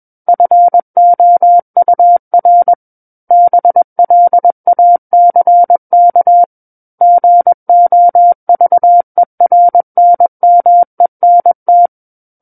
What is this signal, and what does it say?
Morse code, heard as a keyed tone, about 21 wpm, 700 Hz. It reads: FOUR BLACK GOVERNMENT
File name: Book One / 12-Chapter 12 - 21 WPM